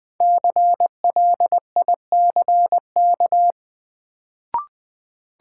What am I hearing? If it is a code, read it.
Morse code, heard as a keyed tone, about 20 wpm, 700 Hz. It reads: CLICK
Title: Sentences from Top 1100 Words - New Word Drill - 20wpm